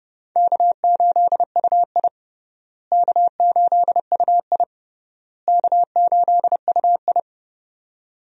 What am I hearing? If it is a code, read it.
Morse code, heard as a keyed tone, about 31 wpm, 700 Hz. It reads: K8UI K8UI K8UI
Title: USA Call Signs - 4 Characters - Repeated 3x - 30wpm